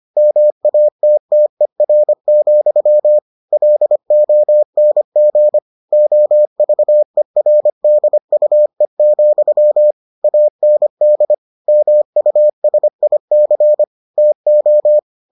Morse code, heard as a keyed tone, about 25 wpm, 600 Hz. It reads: MATTER, LONG OVERDUE, AND MUSIC TO